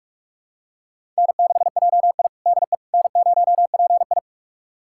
Morse code, heard as a keyed tone, about 44 wpm, 690 Hz. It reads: N6JI DE N0PI
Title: USA Callsign DE Callsign - 4 Characters - 45wpm